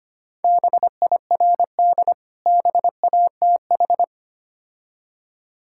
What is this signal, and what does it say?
Morse code, heard as a keyed tone, about 25 wpm, 700 Hz. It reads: BIRD BATH
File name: practice / 25wpm / Difficult Words with Hints - 25wpm